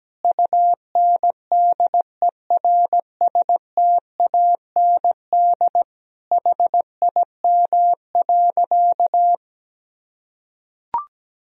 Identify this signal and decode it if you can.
Morse code, heard as a keyed tone, about 17 wpm, 700 Hz. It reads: UNDERSTAND HIM.
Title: Sentences from Top 1000 Words - Review Part 1 - 17wpm